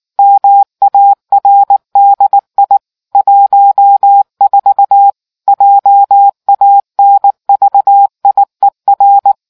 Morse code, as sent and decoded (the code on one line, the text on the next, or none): -- .- .-. -.. .. / .---- ....- / .--- .- -. ...- .. .-.
MARDI 14 JANVIER